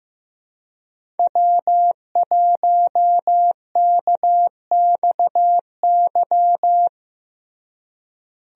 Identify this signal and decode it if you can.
Morse code, heard as a keyed tone, about 15 wpm, 700 Hz. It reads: W1KXY
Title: USA Call Signs - 5 Characters - Repeated 3x - 15wpm